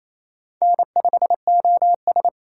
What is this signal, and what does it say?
Morse code, heard as a keyed tone, about 34 wpm, 700 Hz. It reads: N5OS
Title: Speed Racing - USA 1x2 Call Signs 28wpm